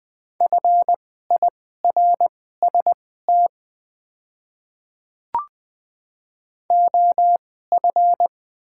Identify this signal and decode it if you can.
Morse code, heard as a keyed tone, about 20 wpm, spaced as for 14 wpm, 700 Hz. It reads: FIRST OF
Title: 2x Character Spacing - Top 200 Words - 20wpm